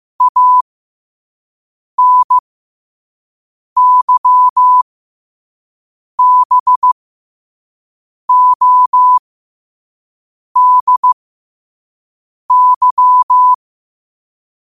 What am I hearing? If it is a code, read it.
Morse code, heard as a keyed tone, about 15 wpm, spaced as for 5 wpm, 1000 Hz. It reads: ANYBODY